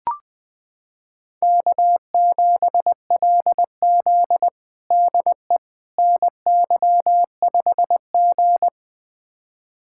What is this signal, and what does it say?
Morse code, heard as a keyed tone, about 20 wpm, 700 Hz. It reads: K7LZ DE NY5G
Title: USA Callsign DE Callsign - 4 Characters - 20wpm